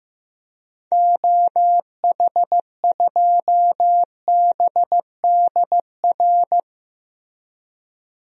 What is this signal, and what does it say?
Morse code, heard as a keyed tone, about 15 wpm, 700 Hz. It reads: OH2BDR
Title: DXCC-100 Call Signs - 6 Characters - Repeated 2x - 15wpm